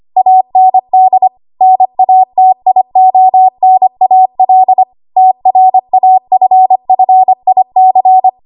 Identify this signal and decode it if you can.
Morse code, heard as a keyed tone, about 25 wpm, 750 Hz. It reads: AND NATIONAL TRAFFIC
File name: Archive / 25 WPM / 25WPM